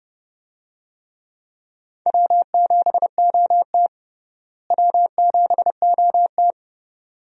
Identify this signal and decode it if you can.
Morse code, heard as a keyed tone, about 31 wpm, 700 Hz. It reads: W7OT W7OT
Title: USA Call Signs - 4 Characters - Repeated 2x - 30wpm